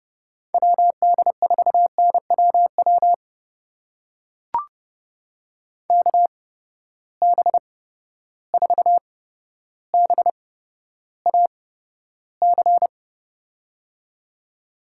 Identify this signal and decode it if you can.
Morse code, heard as a keyed tone, about 30 wpm, spaced as for 13 wpm, 700 Hz. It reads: WD4NWW KB4BAC
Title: USA Call Signs - 6 Characters - 8x Character Spacing - 30wpm